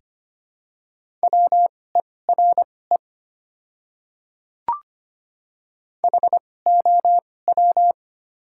Morse code, heard as a keyed tone, about 25 wpm, spaced as for 18 wpm, 690 Hz. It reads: WERE HOW
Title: 2x Character Spacing - Top 100 Words - 25wpm